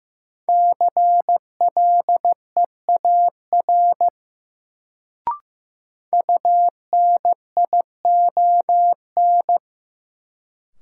Morse code, heard as a keyed tone, about 15 wpm, 700 Hz. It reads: CLEAR UNION
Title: Sets of 5 Words Encore - New Word Drill - 15wpm